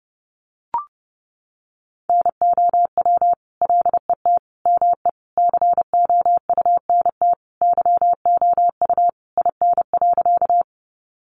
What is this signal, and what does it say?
Morse code, heard as a keyed tone, about 30 wpm, 700 Hz. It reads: NOW LET ME COUNT YOU IN.
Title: Sentences from Top 1500 Words 30wpm